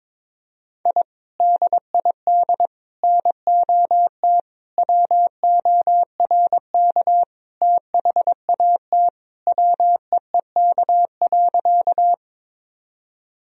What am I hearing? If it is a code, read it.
Morse code, heard as a keyed tone, about 22 wpm, 700 Hz. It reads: I DID NOT WORK THAT WEEK.
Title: Sentences from Top 1000 Words - Review Part 2 - 22wpm